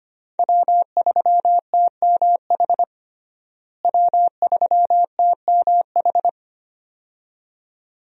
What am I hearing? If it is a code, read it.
Morse code, heard as a keyed tone, about 25 wpm, 700 Hz. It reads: W3TMH W3TMH
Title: USA Call Signs - 5 Characters - Repeated 2x - 25wpm